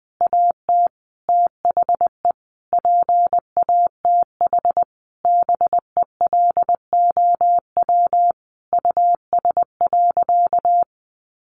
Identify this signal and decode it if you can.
Morse code, heard as a keyed tone, about 19 wpm, 700 Hz. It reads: AT THE PATH BELOW US.
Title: Sentences from Top 1400 Words 20wpm